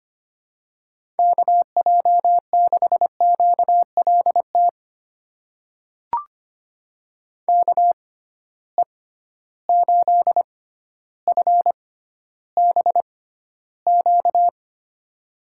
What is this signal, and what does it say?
Morse code, heard as a keyed tone, about 25 wpm, spaced as for 13 wpm, 700 Hz. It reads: KJ6QLT KE8FBQ
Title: USA Call Signs - 6x Character Spacing - 25wpm